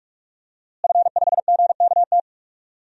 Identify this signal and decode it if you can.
Morse code, heard as a keyed tone, about 41 wpm, 700 Hz. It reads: F5GKT
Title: DXCC-100 Call Signs - 5 Characters - Repeated 3x - 45wpm